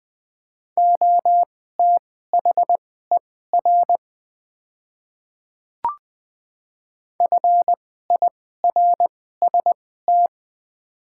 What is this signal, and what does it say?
Morse code, heard as a keyed tone, about 20 wpm, spaced as for 14 wpm, 700 Hz. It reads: OTHER FIRST
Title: 2x Character Spacing - Top 100 Words - 20wpm